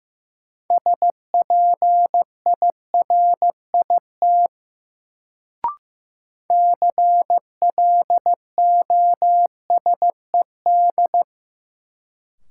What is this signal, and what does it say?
Morse code, heard as a keyed tone, about 15 wpm, 700 Hz. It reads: SPIRIT CLOSED
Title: Sentences from Top 1400 Words - New Word Drill Alt - 15wpm